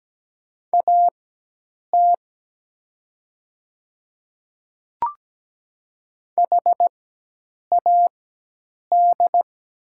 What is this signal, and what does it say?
Morse code, heard as a keyed tone, about 17 wpm, spaced as for 8 wpm, 700 Hz. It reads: AT HAD
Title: 4x Character Spacing - Top 100 Words - 17wpm